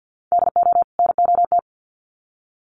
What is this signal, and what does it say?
Morse code, heard as a keyed tone, about 50 wpm, 700 Hz. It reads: DO NOT
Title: Sets of 2 Words 50wpm